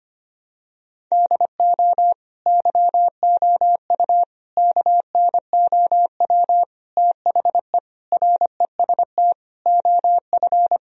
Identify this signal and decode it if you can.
Morse code, heard as a keyed tone, about 25 wpm, 700 Hz. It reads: DO YOU KNOW THE REST OF
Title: Sentences from Top 1000 Words - Review Part 2 - 25wpm